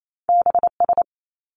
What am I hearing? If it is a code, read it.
Morse code, heard as a keyed tone, about 28 wpm, 700 Hz. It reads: B S
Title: Top Two Letter Combinations in Words 28wpm